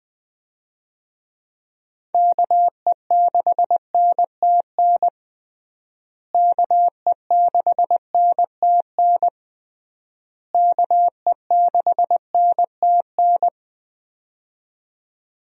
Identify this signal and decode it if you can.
Morse code, heard as a keyed tone, about 20 wpm, 700 Hz. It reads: KE6NTN KE6NTN KE6NTN
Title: USA Call Signs - 6 Characters - Repeated 3x - 20wpm